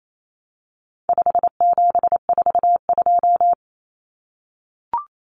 Morse code, heard as a keyed tone, about 28 wpm, 700 Hz. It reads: 5742
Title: Common Cut Numbers - 4-Digit 28wpm